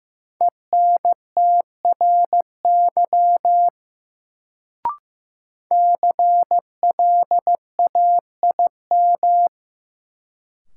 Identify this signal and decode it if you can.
Morse code, heard as a keyed tone, about 15 wpm, 700 Hz. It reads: ENTRY CLAIM
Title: Sentences from Top 1500 Words - New Word Drill - 15wpm